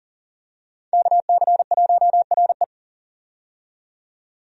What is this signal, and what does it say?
Morse code, heard as a keyed tone, about 40 wpm, 700 Hz. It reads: KC1RE
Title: USA Call Signs - 5 Characters - 40wpm